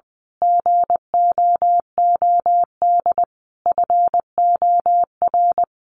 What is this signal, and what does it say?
Morse code, heard as a keyed tone, about 20 wpm, 700 Hz. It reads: GOOD FOR
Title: Sentences from Top 1400 Words 20wpm